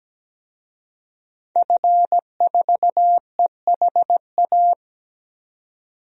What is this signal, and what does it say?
Morse code, heard as a keyed tone, about 17 wpm, 700 Hz. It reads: F4EHA
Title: DXCC-100 Call Signs - 5 Characters - 17wpm